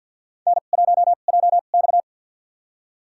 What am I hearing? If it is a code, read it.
Morse code, heard as a keyed tone, about 49 wpm, spaced as for 36 wpm, 700 Hz. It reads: N1JX
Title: USA Call Signs - 2x Character Spacing - 50wpm